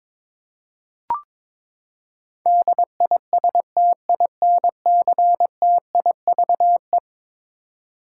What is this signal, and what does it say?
Morse code, heard as a keyed tone, about 22 wpm, 700 Hz. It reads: DISTINCTIVE
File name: practice / 22wpm / Eleven Letter Words - 22wpm